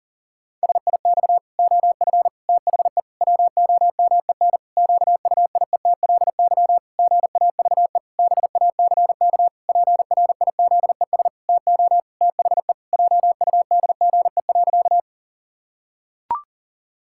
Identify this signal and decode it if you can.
Morse code, heard as a keyed tone, about 40 wpm, 700 Hz. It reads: SIX OF THE WOMEN QUIETLY GAVE BACK PRIZES TO THE JUDGE.
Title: Pangram Challenge v2 - 40wpm